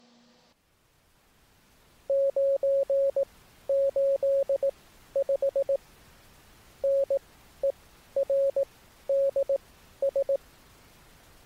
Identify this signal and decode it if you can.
Morse code, heard as a keyed tone, about 18 wpm, spaced as for 11 wpm, 550 Hz. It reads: 985NERDS